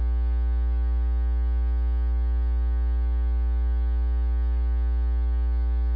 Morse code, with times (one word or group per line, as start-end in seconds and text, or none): none